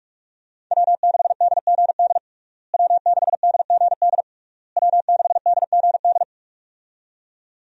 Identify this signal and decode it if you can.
Morse code, heard as a keyed tone, about 41 wpm, 700 Hz. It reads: W6DGD W6DGD W6DGD
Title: USA Call Signs - 5 Characters - Repeated 3x - 45wpm